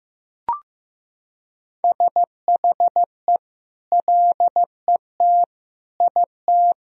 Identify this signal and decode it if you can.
Morse code, heard as a keyed tone, about 15 wpm, 700 Hz. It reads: SHE LET IT
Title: Sentences from Top 1000 Words - Review Part 1 - 15wpm